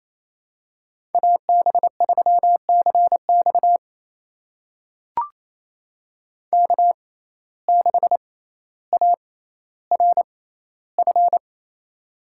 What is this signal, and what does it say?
Morse code, heard as a keyed tone, about 28 wpm, spaced as for 14 wpm, 700 Hz. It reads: AB3CX K6ARF
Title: USA Call Signs - 5 Characters - 6x Character Spacing - 28wpm